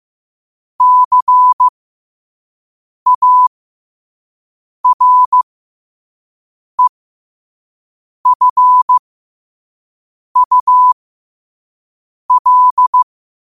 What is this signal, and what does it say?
Morse code, heard as a keyed tone, about 15 wpm, spaced as for 5 wpm, 1000 Hz. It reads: CAREFUL
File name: qotdfiles / 5wpm